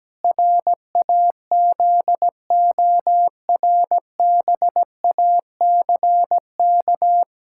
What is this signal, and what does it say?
Morse code, heard as a keyed tone, about 17 wpm, 700 Hz. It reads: RAZORBACK
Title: Pangram Challenge v2 - 17wpm